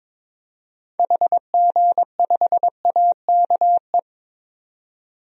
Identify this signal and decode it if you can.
Morse code, heard as a keyed tone, about 22 wpm, 700 Hz. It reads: HG5AKE
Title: DXCC-100 Call Signs - 6 Characters - 22wpm